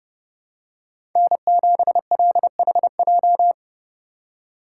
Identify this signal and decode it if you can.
Morse code, heard as a keyed tone, about 30 wpm, 700 Hz. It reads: N7LHJ
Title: USA Call Signs - 5 Characters - 2x Character Spacing - 30wpm